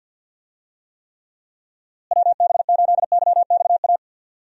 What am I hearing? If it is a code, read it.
Morse code, heard as a keyed tone, about 50 wpm, 700 Hz. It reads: WB8YXA
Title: USA Call Signs - 6 Characters - Repeated 2x - 50wpm